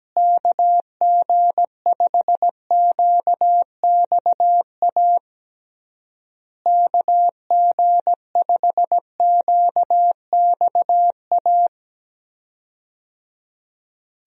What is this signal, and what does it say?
Morse code, heard as a keyed tone, about 17 wpm, 700 Hz. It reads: KG5QXA KG5QXA
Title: USA Call Signs - 6 Characters - Repeated 3x - 17wpm